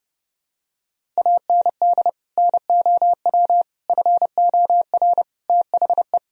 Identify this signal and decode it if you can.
Morse code, heard as a keyed tone, about 30 wpm, 700 Hz. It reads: AND NOW FOR THE
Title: Sentences from Top 1000 Words - Review Part 1 - 30wpm